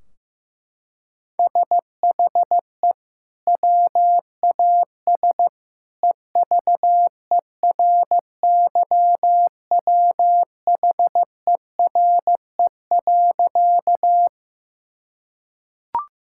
Morse code, heard as a keyed tone, about 15 wpm, 700 Hz. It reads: SHE WAS EVERYWHERE.
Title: Sentences from Top 1600 Words 15wpm